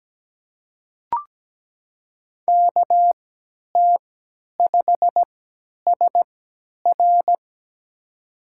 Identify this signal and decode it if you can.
Morse code, heard as a keyed tone, about 17 wpm, spaced as for 10 wpm, 700 Hz. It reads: KT5SR
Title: USA Call Signs - 5 Characters - 3x Character Spacing - 17wpm